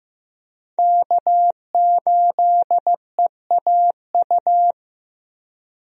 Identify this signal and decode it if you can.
Morse code, heard as a keyed tone, about 15 wpm, 700 Hz. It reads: K8EAU
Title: USA Call Signs - 5 Characters - Repeated 2x - 15wpm